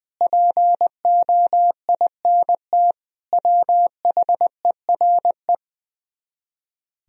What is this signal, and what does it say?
Morse code, heard as a keyed tone, about 20 wpm, 700 Hz. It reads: POINT WHERE
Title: Speed-Racing - Sets of 4 Words 20wpm